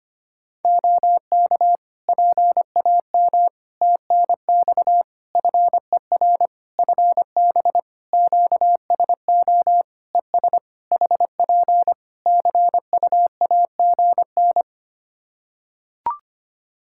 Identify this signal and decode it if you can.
Morse code, heard as a keyed tone, about 25 wpm, 700 Hz. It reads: OK PAM TNX FER FB QSO ES HP CUAGN
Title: K1USN SST QSO Practice 25wpm